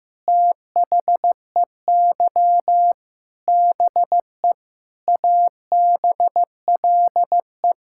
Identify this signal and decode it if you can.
Morse code, heard as a keyed tone, about 15 wpm, 700 Hz. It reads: THEY BE ABLE
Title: Sentences from Top 1000 Words - Review Part 5 - 15wpm